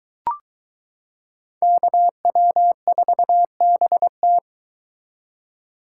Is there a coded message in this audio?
Yes: Morse code from 1.6 to 4.4 s, heard as a keyed tone, about 22 wpm, 700 Hz. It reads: KW4BT